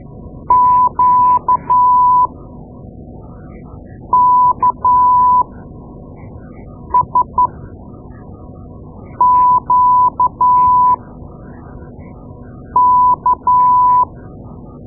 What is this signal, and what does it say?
Morse code, heard as a keyed tone, about 12 wpm, spaced as for 4 wpm, 1000 Hz. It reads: QKSQK
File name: Morse Lessons / Morse Lessons / Tape-2A